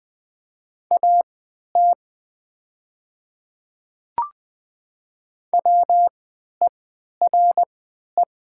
Morse code, heard as a keyed tone, about 20 wpm, spaced as for 11 wpm, 700 Hz. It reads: AT WERE
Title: 3x Character Spacing - Top 100 Words - 20wpm